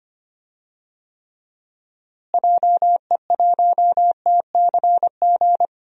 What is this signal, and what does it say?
Morse code, heard as a keyed tone, about 25 wpm, 700 Hz. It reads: JE1TCG
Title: DXCC-100 Call Signs - 6 Characters - Repeated 3x - 25wpm